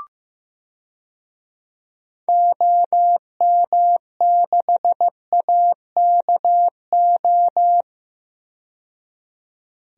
Morse code, heard as a keyed tone, about 15 wpm, 700 Hz. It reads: OM6AKO
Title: DXCC-100 Call Signs - 6 Characters - 15wpm